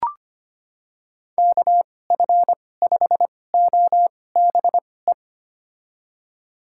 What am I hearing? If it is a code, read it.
Morse code, heard as a keyed tone, about 25 wpm, spaced as for 18 wpm, 700 Hz. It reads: KF5OBE
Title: USA Call Signs - 6 Characters - 2x Character Spacing - 25wpm